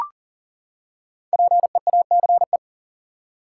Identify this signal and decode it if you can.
Morse code, heard as a keyed tone, about 40 wpm, 700 Hz. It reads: PEACE
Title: Five Letter Words - 40wpm